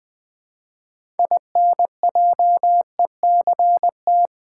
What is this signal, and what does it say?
Morse code, heard as a keyed tone, about 20 wpm, 700 Hz. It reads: INJECT
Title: Build A Word - 3 Letters At A Time - 6-Letter Words - 20wpm